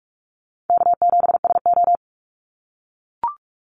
Morse code, heard as a keyed tone, about 44 wpm, 700 Hz. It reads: K7SO
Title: NA SOTA Activator Callsigns 45wpm